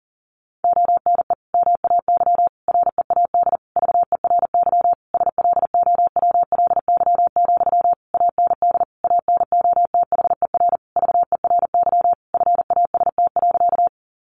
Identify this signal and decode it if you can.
Morse code, heard as a keyed tone, about 37 wpm, 700 Hz. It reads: ONE MAY READ VERY SLOWLY, AND ANOTHER VERY FAST.
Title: Sentences from Top 1100 Words 40wpm